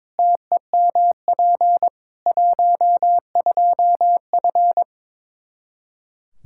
Morse code, heard as a keyed tone, about 22 wpm, 700 Hz. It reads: TEMP 12F
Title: Practice QSO - Exchange 2 - Info - 22wpm